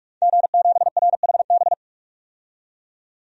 Morse code, heard as a keyed tone, about 47 wpm, 700 Hz. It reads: G7RHB
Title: DXCC-100 Call Signs - 5 Characters - 45wpm